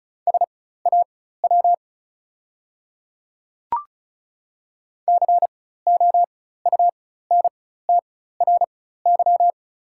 Morse code, heard as a keyed tone, about 36 wpm, spaced as for 16 wpm, 690 Hz. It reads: SAW COUNTRY
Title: 4x Character Spacing - Top 300 Words - 35wpm